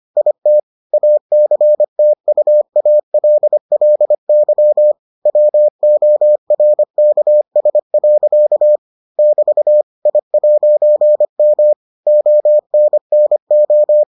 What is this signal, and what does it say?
Morse code, heard as a keyed tone, about 25 wpm, 600 Hz. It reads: IT ACTUALLY WORKS. <BT> I'M ONNO